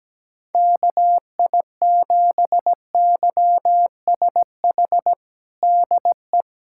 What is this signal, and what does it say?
Morse code, heard as a keyed tone, about 17 wpm, 700 Hz. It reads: KI7YSH DE
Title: USA Callsign DE Callsign 17wpm